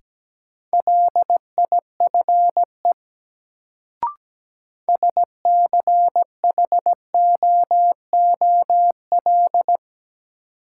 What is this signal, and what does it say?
Morse code, heard as a keyed tone, about 17 wpm, 700 Hz. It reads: LIFE SCHOOL
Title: Sentences from Top 300 Words - New Word Drill - 17wpm